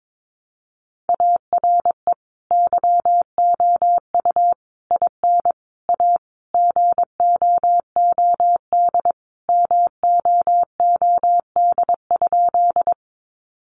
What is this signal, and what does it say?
Morse code, heard as a keyed tone, about 22 wpm, 700 Hz. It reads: ARE YOU IN A GOOD MOOD?
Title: Sentences from Top 1100 Words 22wpm